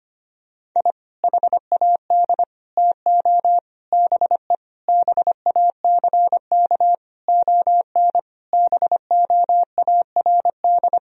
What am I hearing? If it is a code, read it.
Morse code, heard as a keyed tone, about 25 wpm, 700 Hz. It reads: I HAD TO BE BACK ON BOARD
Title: Sentences from Top 1000 Words - Review Part 4 - 25wpm